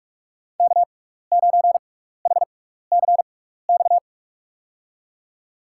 Morse code, heard as a keyed tone, about 43 wpm, spaced as for 15 wpm, 700 Hz. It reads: K9HCX
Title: USA Call Signs - 5 Characters - 6x Character Spacing - 45wpm